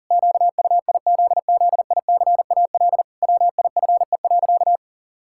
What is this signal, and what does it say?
Morse code, heard as a keyed tone, about 40 wpm, 700 Hz. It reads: QUIZZICAL WIFE.
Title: Pangram Challenge v2 - 40wpm